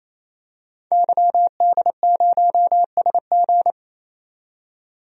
Morse code, heard as a keyed tone, about 28 wpm, 700 Hz. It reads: YD0SG
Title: DXCC-100 Call Signs - 5 Characters - 28wpm